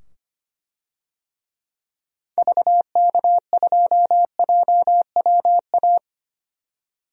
Morse code, heard as a keyed tone, about 25 wpm, 690 Hz. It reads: VK2JWA
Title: DXCC-100 Call Signs - 6 Characters - 25wpm